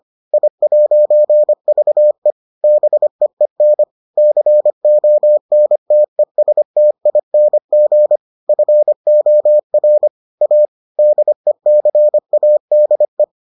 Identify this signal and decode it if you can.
Morse code, heard as a keyed tone, about 25 wpm, 600 Hz. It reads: I'VE BEEN CONTESTING FOR A DECADE